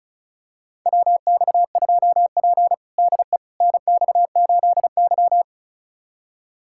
Morse code, heard as a keyed tone, about 34 wpm, 700 Hz. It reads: WX2P DE NX8Y
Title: USA Callsign DE Callsign - 4 Characters - 35wpm